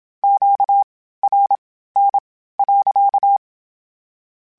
Morse code, heard as a keyed tone, about 27 wpm, spaced as for 15 wpm, 800 Hz. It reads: QRN.
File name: Q Calls / Q Calls-mp3s / Q Calls sppc @25